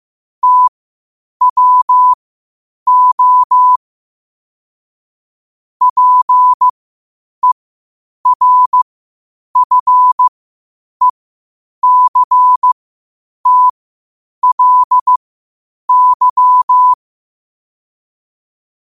Morse code, heard as a keyed tone, about 15 wpm, spaced as for 8 wpm, 1000 Hz. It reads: TWO PERFECTLY
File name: qotdfiles / 8wpm